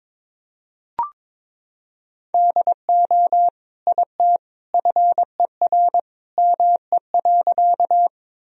Morse code, heard as a keyed tone, about 21 wpm, 700 Hz. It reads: DO IT FER ME.
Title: Sentences from Top 200 Words and Abbreviations 22wpm